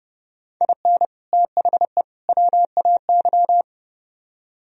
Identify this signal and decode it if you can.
Morse code, heard as a keyed tone, about 34 wpm, 700 Hz. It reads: IN THE WAY
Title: Speed-Racing - Sets of 3 Words 30wpm